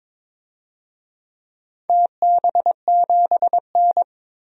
Morse code, heard as a keyed tone, about 22 wpm, 700 Hz. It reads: TB7N
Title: DXCC-100 Call Signs - 4 Characters - 22wpm